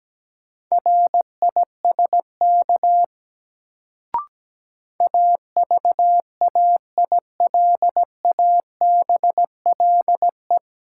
Morse code, heard as a keyed tone, about 17 wpm, 700 Hz. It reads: RISK AVAILABLE
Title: Sentences from Top 600 Words - New Word Drill - 17wpm